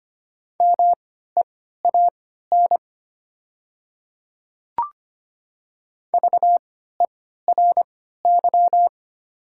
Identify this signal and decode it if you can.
Morse code, heard as a keyed tone, about 25 wpm, spaced as for 14 wpm, 690 Hz. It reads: MEAN VERY
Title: 3x Character Spacing - Top 200 Words - 25wpm